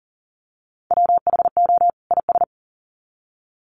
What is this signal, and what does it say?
Morse code, heard as a keyed tone, about 40 wpm, 700 Hz. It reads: WHO IS
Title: Sets of 2 Words 40wpm